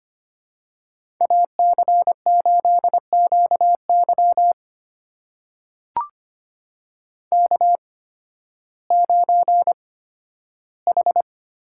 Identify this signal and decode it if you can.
Morse code, heard as a keyed tone, about 25 wpm, spaced as for 13 wpm, 700 Hz. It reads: AC8QY K9H
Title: USA Call Signs - 8x Character Spacing - 25wpm